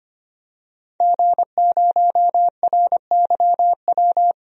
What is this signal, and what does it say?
Morse code, heard as a keyed tone, about 25 wpm, 700 Hz. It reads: G0RYW
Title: DXCC-100 Call Signs - 5 Characters - 25wpm